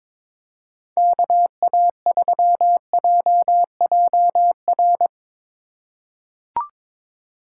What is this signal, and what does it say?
Morse code, heard as a keyed tone, about 21 wpm, 700 Hz. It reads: KA3JJR
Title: USA Call Signs - 6 Characters - 4x Character Spacing - 22wpm